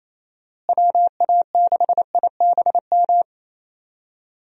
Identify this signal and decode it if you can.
Morse code, heard as a keyed tone, about 27 wpm, 700 Hz. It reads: WA6IBM
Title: USA Call Signs - 6 Characters - Repeated 2x - 28wpm